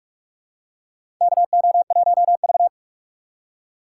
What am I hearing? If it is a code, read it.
Morse code, heard as a keyed tone, about 46 wpm, 700 Hz. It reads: KO1V
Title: USA Call Signs - 3x Character Spacing - 45wpm